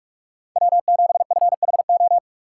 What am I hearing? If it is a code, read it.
Morse code, heard as a keyed tone, about 46 wpm, 700 Hz. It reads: W7FHO
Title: USA Call Signs - 5 Characters - 3x Character Spacing - 45wpm